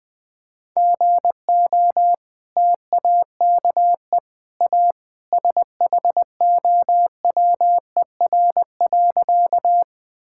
Morse code, heard as a keyed tone, about 20 wpm, 700 Hz. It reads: GO TAKE A SHOWER.
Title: Sentences from Top 1600 Words 20wpm